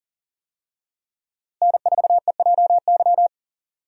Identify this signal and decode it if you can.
Morse code, heard as a keyed tone, about 40 wpm, 700 Hz. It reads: N4EJY
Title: USA Call Signs - 5 Characters - Repeated 3x - 40wpm